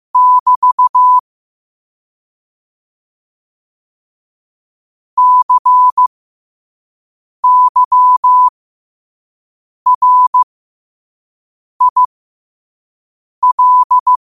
Morse code, heard as a keyed tone, about 15 wpm, spaced as for 5 wpm, 1000 Hz. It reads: <BT> CYRIL
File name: qotdfiles / 5wpm